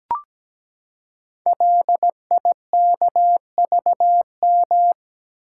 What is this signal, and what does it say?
Morse code, heard as a keyed tone, about 17 wpm, 700 Hz. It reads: LIKVM
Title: Code Groups - 5 Letters - 17wpm